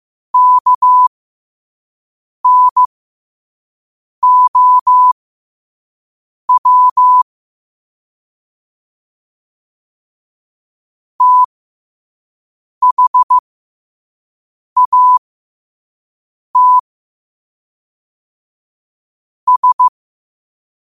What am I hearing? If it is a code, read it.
Morse code, heard as a keyed tone, about 15 wpm, spaced as for 5 wpm, 1000 Hz. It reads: KNOW THAT S
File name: qotdfiles / 5wpm